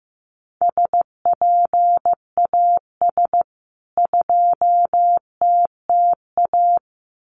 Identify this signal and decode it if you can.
Morse code, heard as a keyed tone, about 15 wpm, 700 Hz. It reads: SPAS 2TTA
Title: CWT Exchange 15wpm